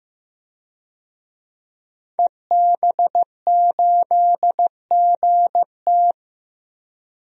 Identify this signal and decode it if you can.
Morse code, heard as a keyed tone, about 15 wpm, 700 Hz. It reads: EB8GT
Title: DXCC-100 Call Signs - 5 Characters - 15wpm